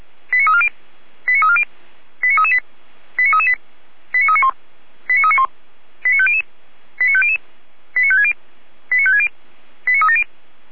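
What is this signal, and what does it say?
Five-tone selective call (ZVEI), heard as quick runs of five tones, tones of about 70 ms. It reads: 89340 89340 89338 89338 89391 89391 89500 89500 89560 89560 89370
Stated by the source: ZVEI five-tone call 89304 received by SDR.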